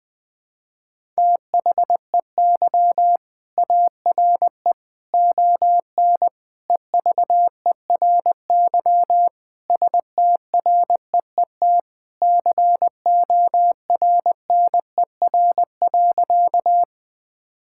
Morse code, heard as a keyed tone, about 20 wpm, 700 Hz. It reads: THEY ARE ON EVERY STREET CORNER.
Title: Sentences from Top 1600 Words 20wpm